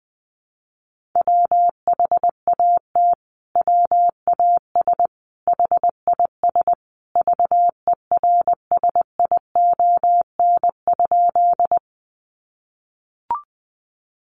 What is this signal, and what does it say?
Morse code, heard as a keyed tone, about 19 wpm, 700 Hz. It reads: WHAT WAS HIS VERSION?
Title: Sentences from Top 1400 Words 20wpm